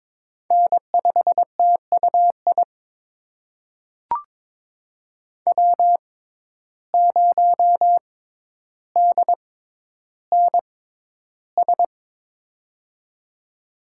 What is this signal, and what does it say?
Morse code, heard as a keyed tone, about 22 wpm, spaced as for 11 wpm, 700 Hz. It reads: N5TUI W0DNS
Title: USA Call Signs - 5 Characters - 6x Character Spacing - 22wpm